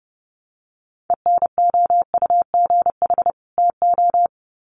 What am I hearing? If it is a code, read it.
Morse code, heard as a keyed tone, about 30 wpm, 700 Hz. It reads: ENOUGH TO
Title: Sets of 2 Words 30wpm